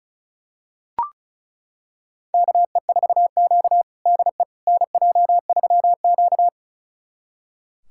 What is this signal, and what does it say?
Morse code, heard as a keyed tone, about 36 wpm, 700 Hz. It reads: KE4Q DE NJ3Q
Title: USA Callsign DE Callsign - 4 Characters - 35wpm